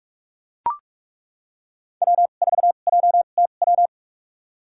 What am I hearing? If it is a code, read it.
Morse code, heard as a keyed tone, about 42 wpm, spaced as for 31 wpm, 700 Hz. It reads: W4JTW
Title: USA Call Signs - 2x Character Spacing - 45wpm